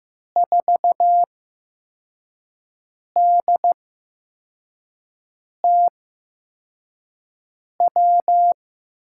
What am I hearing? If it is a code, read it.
Morse code, heard as a keyed tone, about 15 wpm, spaced as for 4 wpm, 700 Hz. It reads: 4DTW
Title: USA Call Signs - 6 Characters - 8x Character Spacing - 15wpm